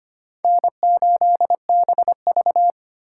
Morse code, heard as a keyed tone, about 32 wpm, 700 Hz. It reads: N8BV